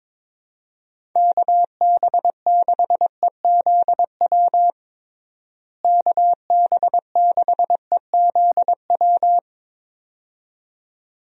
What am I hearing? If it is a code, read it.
Morse code, heard as a keyed tone, about 22 wpm, 700 Hz. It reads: KB6EZW KB6EZW
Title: USA Call Signs - 6 Characters - Repeated 2x - 22wpm